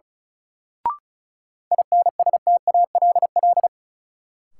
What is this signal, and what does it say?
Morse code, heard as a keyed tone, about 36 wpm, 700 Hz. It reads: INSTALL
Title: Sentences from Top 1600 Words - New Word Drill Alt - 35wpm